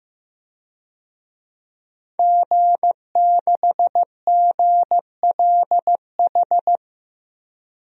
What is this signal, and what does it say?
Morse code, heard as a keyed tone, about 15 wpm, 700 Hz. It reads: G6GLH
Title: DXCC-100 Call Signs - 5 Characters - 15wpm